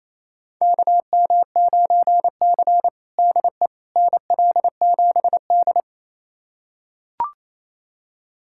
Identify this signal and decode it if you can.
Morse code, heard as a keyed tone, about 28 wpm, 690 Hz. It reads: KM9C DE NL7D
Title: USA Callsign DE Callsign - 4 Characters - 28wpm